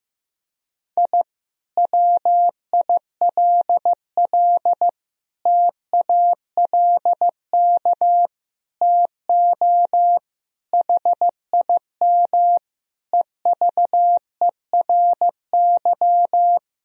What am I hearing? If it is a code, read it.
Morse code, heard as a keyed tone, about 16 wpm, 700 Hz. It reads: I WILL TALK TO HIM EVERY
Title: Sentences from Top 1000 Words - Review Part 2 - 15wpm